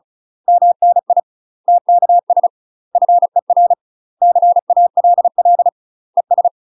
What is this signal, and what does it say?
Morse code, heard as a keyed tone, about 34 wpm, 700 Hz. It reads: MNI TKS FER CALL ES
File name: UK MASTER QSOs TRAINING / 35WPM / UK MASTER QSOs 0004